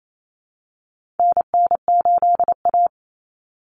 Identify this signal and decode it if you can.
Morse code, heard as a keyed tone, about 28 wpm, 700 Hz. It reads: NN8A
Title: Common Cut Numbers - 4-Digit 28wpm